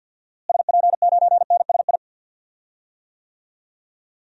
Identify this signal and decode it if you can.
Morse code, heard as a keyed tone, about 53 wpm, 700 Hz. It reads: SP9NSI
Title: DXCC-100 Call Signs - 6 Characters - Repeated 2x - 50wpm